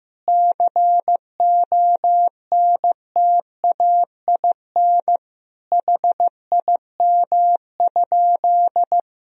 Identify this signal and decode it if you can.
Morse code, heard as a keyed tone, about 15 wpm, 700 Hz. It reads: CONTAIN HIM?
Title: Sentences from Top 1000 Words - Review Part 2 - 15wpm